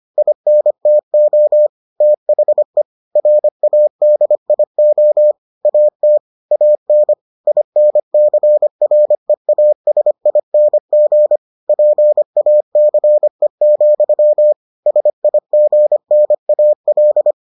Morse code, heard as a keyed tone, about 25 wpm, 600 Hz. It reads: INTO THE RADIO AT AN INCREASING PACE, SIGNAL